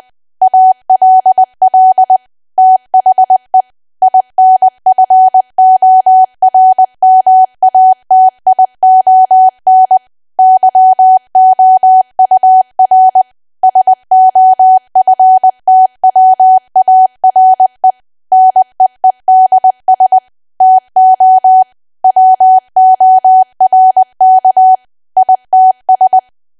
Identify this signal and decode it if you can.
Morse code, heard as a keyed tone, about 20 wpm, 750 Hz. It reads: ALL THE INFORMATION YOUR SOFTWARE NEEDS TO WORK ITS
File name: cwa / W1AW-Level3 / 20WPM